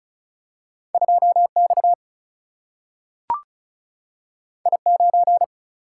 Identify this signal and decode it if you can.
Morse code, heard as a keyed tone, about 36 wpm, 700 Hz. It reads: 2 X I 9
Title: Single-Letter Single-Number 35wpm